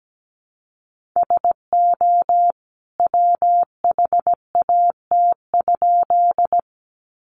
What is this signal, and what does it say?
Morse code, heard as a keyed tone, about 17 wpm, 700 Hz. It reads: SO WHAT?
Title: A Sound Of Thunder 17wpm